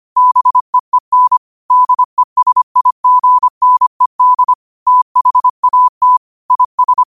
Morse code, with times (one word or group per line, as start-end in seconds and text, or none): 0.2-1.4 BEEN
1.7-4.5 DESIGNED
4.9-6.2 THAT
6.5-7.0 IS